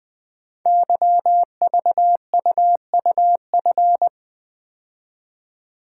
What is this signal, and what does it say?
Morse code, heard as a keyed tone, about 20 wpm, 700 Hz. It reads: YVUUF
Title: Code Groups - 5 Letter:Numbers - 20wpm